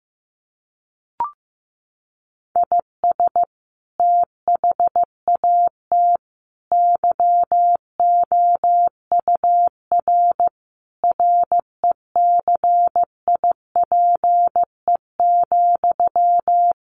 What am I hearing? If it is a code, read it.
Morse code, heard as a keyed tone, about 15 wpm, 700 Hz. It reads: IS THAT YOUR RECIPE,
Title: Sentences from Top 1200 Words 15wpm